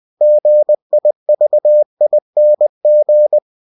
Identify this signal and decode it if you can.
Morse code, heard as a keyed tone, about 20 wpm, 600 Hz. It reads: GIVING
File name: cw-books / winnie-the-pooh / winnie-20wpm-page-0058